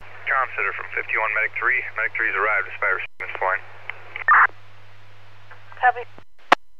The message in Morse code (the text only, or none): none